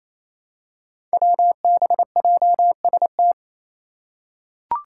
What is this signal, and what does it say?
Morse code, heard as a keyed tone, about 28 wpm, 700 Hz. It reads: WBJST
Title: Five Challenging Letter-Number Combinations 28wpm